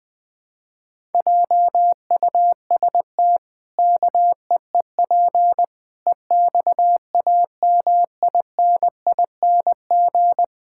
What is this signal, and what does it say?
Morse code, heard as a keyed tone, about 20 wpm, 700 Hz. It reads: JUST KEEP EXAMINING
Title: Pangram Challenge v2 - 20wpm